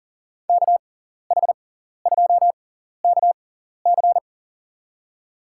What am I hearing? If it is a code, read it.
Morse code, heard as a keyed tone, about 40 wpm, spaced as for 14 wpm, 700 Hz. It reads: KH2KC
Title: USA Call Signs - 5 Characters - 6x Character Spacing - 40wpm